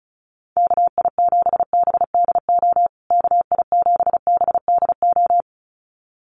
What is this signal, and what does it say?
Morse code, heard as a keyed tone, about 34 wpm, 700 Hz. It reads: KI7BDO KI7BDO
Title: USA Call Signs Repeated 2x 35wpm